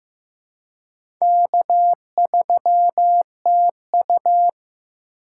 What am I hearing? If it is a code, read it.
Morse code, heard as a keyed tone, about 15 wpm, 700 Hz. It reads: K3TU
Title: USA Call Signs - 4 Characters - 2x Character Spacing - 15wpm